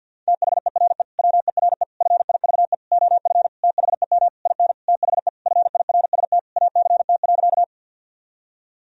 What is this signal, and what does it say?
Morse code, heard as a keyed tone, about 52 wpm, 700 Hz. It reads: THERE WERE FIVE OF THEM IN THE FIRST ACT.
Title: Sentences from Top 1000 Words - Review Part 2 - 50wpm